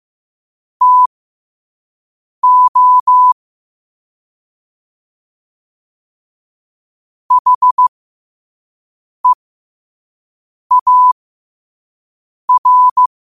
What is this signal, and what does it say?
Morse code, heard as a keyed tone, about 15 wpm, spaced as for 5 wpm, 1000 Hz. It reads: TO HEAR
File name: qotdfiles / 5wpm